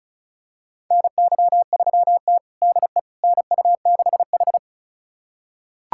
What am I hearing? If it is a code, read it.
Morse code, heard as a keyed tone, about 34 wpm, 700 Hz. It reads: NY3T DE NU6H
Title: USA Callsign DE Callsign - 4 Characters - 35wpm